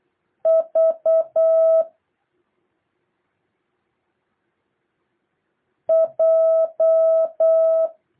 Morse code, heard as a keyed tone, about 8 wpm, 650 Hz. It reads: V J
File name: jh8cbh / 2023cw